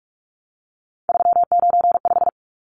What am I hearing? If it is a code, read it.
Morse code, heard as a keyed tone, about 44 wpm, 700 Hz. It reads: 395
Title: Numbers - 3-Digit 45wpm